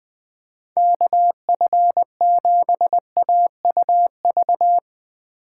Morse code, heard as a keyed tone, about 22 wpm, 700 Hz. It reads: KF7AUV
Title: Speed Racing - USA 2x3 Call Signs 20wpm